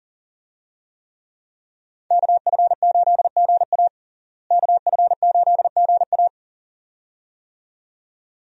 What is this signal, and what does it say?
Morse code, heard as a keyed tone, about 37 wpm, 700 Hz. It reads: KF8GA KF8GA
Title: USA Call Signs - 5 Characters - Repeated 2x - 40wpm